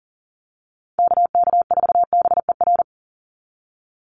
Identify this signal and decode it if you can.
Morse code, heard as a keyed tone, about 40 wpm, 700 Hz. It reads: KK4BER
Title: USA Call Signs 40wpm